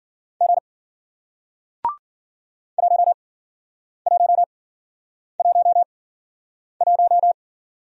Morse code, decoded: G 1 1 1 1

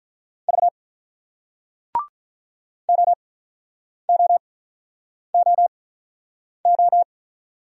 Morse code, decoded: V OOOO

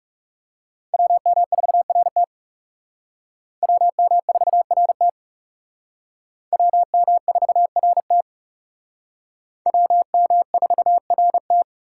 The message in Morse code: WM4RT WM4RT WM4RT WM4RT

